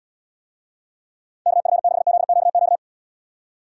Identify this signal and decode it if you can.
Morse code, heard as a keyed tone, about 74 wpm, 700 Hz. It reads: KF6ZQQ